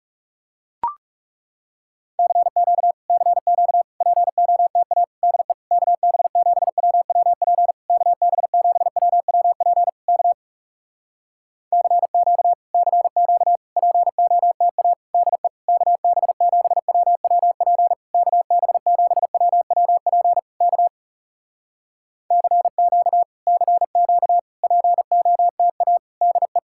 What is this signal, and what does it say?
Morse code, heard as a keyed tone, about 38 wpm, 700 Hz. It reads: CQ CQ POTA DE KB7WWP KB7WWP K CQ CQ POTA DE KB7WWP KB7WWP K CQ CQ POTA DE
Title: Speed-Racing Common QSO Phrases Encore - 30wpm